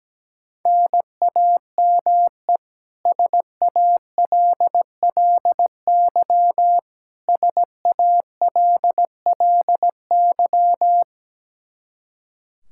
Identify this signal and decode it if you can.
Morse code, heard as a keyed tone, about 17 wpm, 700 Hz. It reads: NAME SALLY SALLY
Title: K1USN SST QSO Practice 17wpm